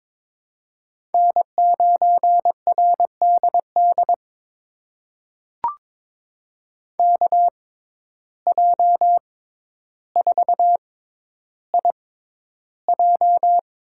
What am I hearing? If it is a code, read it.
Morse code, heard as a keyed tone, about 22 wpm, spaced as for 11 wpm, 700 Hz. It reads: N9RDD KJ4IJ
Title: USA Call Signs - 5 Characters - 6x Character Spacing - 22wpm